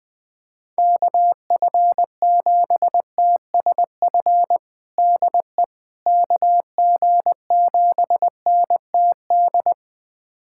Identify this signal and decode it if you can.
Morse code, heard as a keyed tone, about 21 wpm, 700 Hz. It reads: KF7TSF DE KG7NTD
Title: USA Callsign DE Callsign - 6 Characters - 20wpm